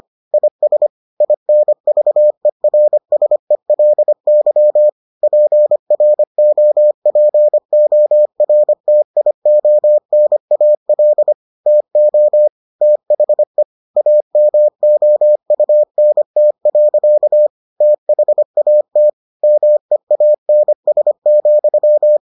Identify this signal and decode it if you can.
Morse code, heard as a keyed tone, about 25 wpm, 600 Hz. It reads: IS INVERSELY PROPORTIONAL TO THE AMOUNT. THAT MEANS,